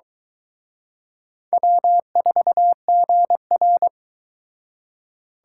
Morse code, heard as a keyed tone, about 26 wpm, 700 Hz. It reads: W4GR